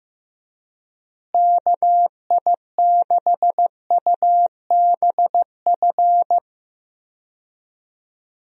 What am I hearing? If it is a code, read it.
Morse code, heard as a keyed tone, about 14 wpm, 700 Hz. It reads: KI6UBF
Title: USA Call Signs - 6 Characters - Repeated 2x - 15wpm